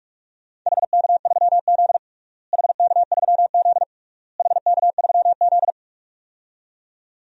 Morse code, heard as a keyed tone, about 46 wpm, 690 Hz. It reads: HK3Z HK3Z HK3Z
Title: DXCC-100 Call Signs - 4 Characters - Repeated 3x - 45wpm